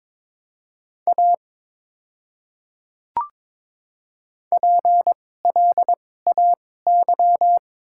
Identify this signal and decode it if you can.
Morse code, heard as a keyed tone, about 22 wpm, spaced as for 16 wpm, 700 Hz. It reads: A PLAY
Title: 2x Character Spacing - Top 200 Words - 22wpm